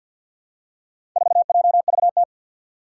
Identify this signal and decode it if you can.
Morse code, heard as a keyed tone, about 52 wpm, 700 Hz. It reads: 4JVT
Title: Four Challenging Letter-Number Combinations 50wpm